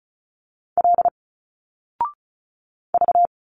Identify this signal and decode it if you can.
Morse code, heard as a keyed tone, about 36 wpm, 700 Hz. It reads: L V
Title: Single Letters 35wpm